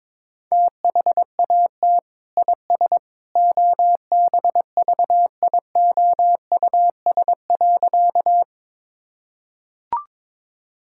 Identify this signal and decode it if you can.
Morse code, heard as a keyed tone, about 22 wpm, 700 Hz. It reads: THAT IS OBVIOUS.
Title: Sentences from Top 1600 Words 22wpm